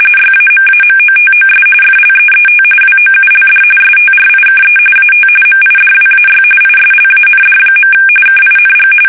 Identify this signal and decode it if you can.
RTTY, about 45.45 baud. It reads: YAMAMOTOS FLEET WOULD STRIKE AT MIDWAY ISLAND. KING
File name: fsk850